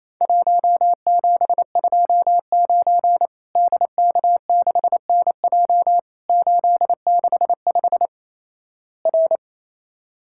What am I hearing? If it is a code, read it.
Morse code, heard as a keyed tone, about 28 wpm, 700 Hz. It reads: 1729 DK6NJ 865 R
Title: Worked All Europe DX Contest - QTC Exchange - 28wpm